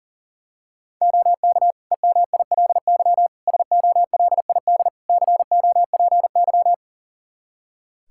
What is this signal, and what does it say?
Morse code, heard as a keyed tone, about 40 wpm, 700 Hz. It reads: OK EMILY SOLID COPY
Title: Practice QSO - Exchange 3 - More Info - 40wpm